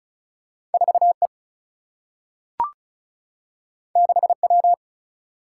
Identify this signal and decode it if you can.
Morse code, heard as a keyed tone, about 36 wpm, 700 Hz. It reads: 4 E 6 W
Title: Single-Letter Single-Number 35wpm